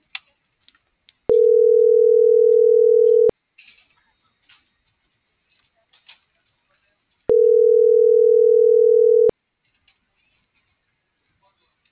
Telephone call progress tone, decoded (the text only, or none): ringback tone